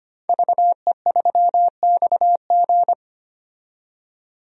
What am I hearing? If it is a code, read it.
Morse code, heard as a keyed tone, about 25 wpm, 700 Hz. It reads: VE3XG
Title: DXCC-100 Call Signs - 5 Characters - 25wpm